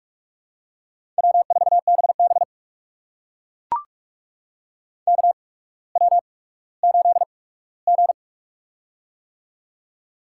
Morse code, heard as a keyed tone, about 43 wpm, spaced as for 19 wpm, 700 Hz. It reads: W4BB KW8G
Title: USA Call Signs - 4 Characters - 8x Character Spacing - 45wpm